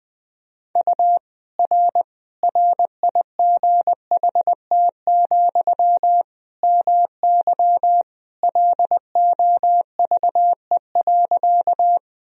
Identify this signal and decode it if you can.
Morse code, heard as a keyed tone, about 20 wpm, 700 Hz. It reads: U R RIGHT, MY LOVE.
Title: Sentences from Top 400 Words and Abbreviations 20wpm